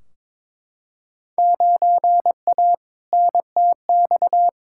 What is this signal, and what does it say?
Morse code, heard as a keyed tone, about 22 wpm, 700 Hz. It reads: 9A NTX
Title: Field Day Exchanges Encore - 22wpm